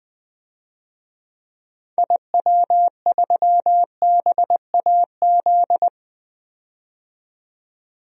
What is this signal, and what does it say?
Morse code, heard as a keyed tone, about 20 wpm, 700 Hz. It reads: IW3BAZ
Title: DXCC-100 Call Signs - 6 Characters - 20wpm